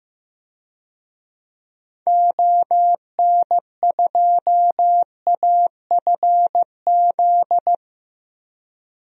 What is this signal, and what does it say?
Morse code, heard as a keyed tone, about 16 wpm, 700 Hz. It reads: ON2AFZ
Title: DXCC-100 Call Signs - 6 Characters - 15wpm